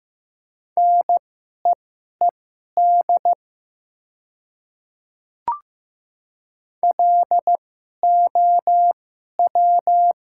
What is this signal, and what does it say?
Morse code, heard as a keyed tone, about 15 wpm, spaced as for 11 wpm, 700 Hz. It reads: NEED LOW
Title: 2x Character Spacing - Top 300 Words - 15wpm